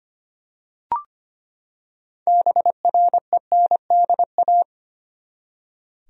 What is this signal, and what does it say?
Morse code, heard as a keyed tone, about 25 wpm, 700 Hz. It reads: BRENDA
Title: USA Common Names - 25wpm